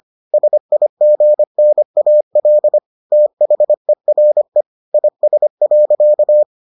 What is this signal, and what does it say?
Morse code, heard as a keyed tone, about 25 wpm, 600 Hz. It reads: SIGNAL THERE IS.